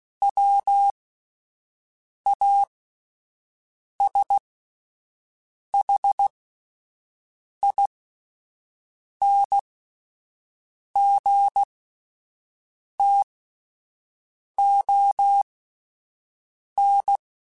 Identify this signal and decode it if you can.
Morse code, heard as a keyed tone, about 16 wpm, spaced as for 5 wpm, 800 Hz. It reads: WASHINGTON
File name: QSOs / QSOs-mp3s / WD0BGP 1@13